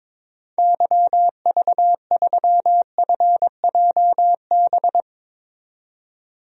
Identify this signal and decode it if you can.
Morse code, heard as a keyed tone, about 22 wpm, 700 Hz. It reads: YV3FJB
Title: DXCC-100 Call Signs - 6 Characters - 22wpm